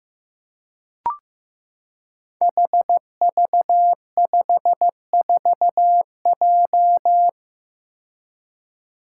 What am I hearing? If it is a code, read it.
Morse code, heard as a keyed tone, about 15 wpm, 700 Hz. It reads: HV54J
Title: Five Challenging Letter-Number Combinations 15wpm